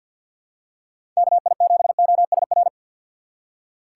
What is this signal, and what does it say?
Morse code, heard as a keyed tone, about 51 wpm, 700 Hz. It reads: KI7OSR
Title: USA Call Signs - 6 Characters - 8x Character Spacing - 50wpm